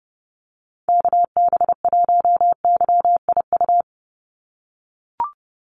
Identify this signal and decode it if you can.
Morse code, heard as a keyed tone, about 30 wpm, 700 Hz. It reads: KB1YIU
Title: USA Call Signs 30wpm